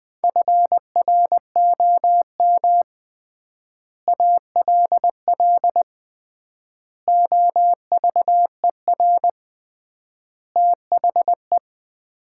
Morse code, heard as a keyed tone, about 20 wpm, 700 Hz. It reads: FROM ALL OVER THE
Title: 3x Word-Spacing - Sets of 5 Words - 20wpm